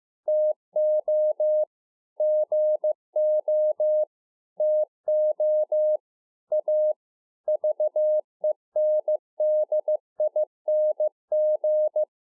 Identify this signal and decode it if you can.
Morse code, heard as a keyed tone, about 15 wpm, 610 Hz. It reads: TO GO TO A VENDING